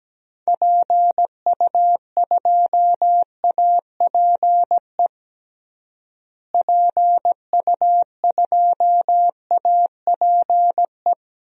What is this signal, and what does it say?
Morse code, heard as a keyed tone, about 17 wpm, 700 Hz. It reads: PU2APE PU2APE
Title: DXCC-100 Call Signs - 6 Characters - Repeated 3x - 17wpm